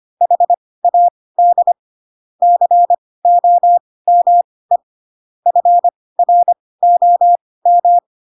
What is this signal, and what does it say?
Morse code, heard as a keyed tone, about 25 wpm, spaced as for 18 wpm, 700 Hz. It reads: HAD COME FROM
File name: Pinocchio / Pinocchio ch1 12 25 ik1hge